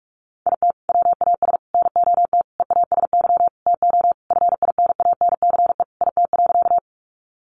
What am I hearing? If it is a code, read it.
Morse code, heard as a keyed tone, about 44 wpm, 700 Hz. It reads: IT WAS NOT EASY TO FINANCE IT.
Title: Sentences from Top 1500 Words 45wpm